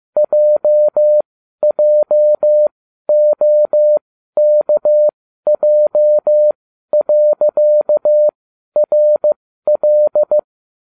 Morse code, heard as a keyed tone, about 15 wpm, spaced as for 12 wpm, 600 Hz. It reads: JJOKJ.RL